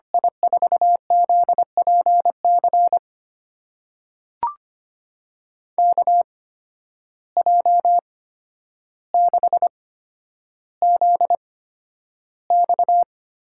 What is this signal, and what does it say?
Morse code, heard as a keyed tone, about 25 wpm, spaced as for 11 wpm, 700 Hz. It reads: I4ZPC KJ6ZX
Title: USA Call Signs - 6 Characters - 8x Character Spacing - 25wpm